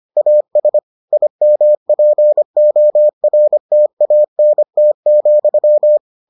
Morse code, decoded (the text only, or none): AS IMPORTANT,